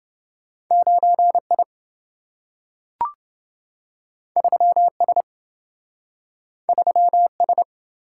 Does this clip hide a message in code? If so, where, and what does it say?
Yes: Morse code from 0.7 to 7.6 s, heard as a keyed tone, about 28 wpm, 700 Hz. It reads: 9I 3S 3S